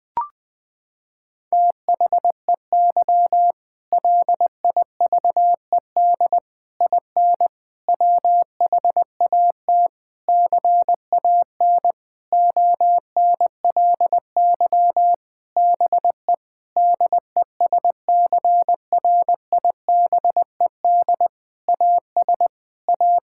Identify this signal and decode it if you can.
Morse code, heard as a keyed tone, about 20 wpm, 700 Hz. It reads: THEY LIVED IN WHAT CAN ONLY BE DESCRIBED AS A
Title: Sentences with 8-Word Phrases - 20wpm